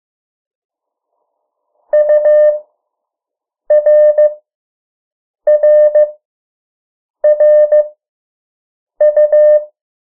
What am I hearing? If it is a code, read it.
Morse code, heard as a keyed tone, about 15 wpm, spaced as for 6 wpm, 600 Hz. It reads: URRRU
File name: morse mp3 / 03 urok 2